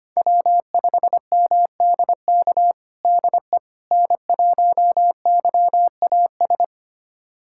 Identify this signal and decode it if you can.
Morse code, heard as a keyed tone, about 25 wpm, 700 Hz. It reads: W5MDK DE N1YAS
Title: USA Callsign DE Callsign - 5 Characters - 25wpm